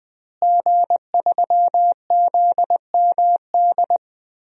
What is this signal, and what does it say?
Morse code, heard as a keyed tone, about 20 wpm, 700 Hz. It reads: G3ZMD
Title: DXCC-100 Call Signs - 5 Characters - Repeated 2x - 20wpm